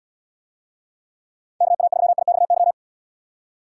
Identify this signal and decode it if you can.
Morse code, heard as a keyed tone, about 78 wpm, 700 Hz. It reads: KA3IXY